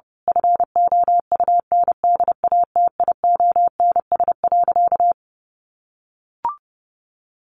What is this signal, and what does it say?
Morse code, heard as a keyed tone, about 30 wpm, 700 Hz. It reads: FOUNDATIONS.
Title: A Sound Of Thunder 30wpm